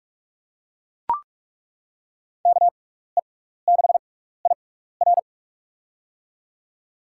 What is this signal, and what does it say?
Morse code, heard as a keyed tone, about 42 wpm, spaced as for 15 wpm, 700 Hz. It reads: KE6IR
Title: USA Call Signs - 5 Characters - 6x Character Spacing - 45wpm